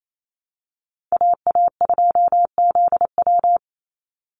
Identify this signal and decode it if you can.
Morse code, heard as a keyed tone, about 28 wpm, 700 Hz. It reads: AA2ZW
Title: Call Signs Repeated 2x - 28wpm